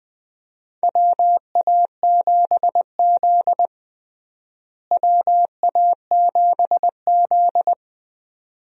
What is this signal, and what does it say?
Morse code, heard as a keyed tone, about 21 wpm, 700 Hz. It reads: WA7Z WA7Z
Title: USA Call Signs - 4 Characters - Repeated 2x - 20wpm